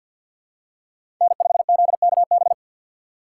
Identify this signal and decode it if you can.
Morse code, heard as a keyed tone, about 47 wpm, 700 Hz. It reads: N5ZKB
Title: USA Call Signs - 5 Characters - 4x Character Spacing - 50wpm